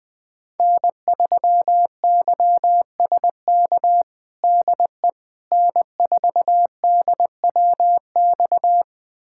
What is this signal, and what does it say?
Morse code, heard as a keyed tone, about 20 wpm, 700 Hz. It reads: N3YSK DE N4DWX
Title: USA Callsign DE Callsign - 5 Characters - 20wpm